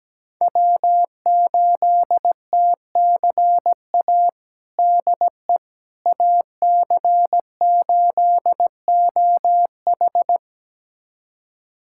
Morse code, heard as a keyed tone, about 17 wpm, 700 Hz. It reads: W8TCA DE AC8OH
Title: USA Callsign DE Callsign - 5 Characters - 17wpm